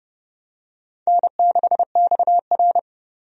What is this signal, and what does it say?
Morse code, heard as a keyed tone, about 30 wpm, 700 Hz. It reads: N6XR
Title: USA Call Signs - 4 Characters - 30wpm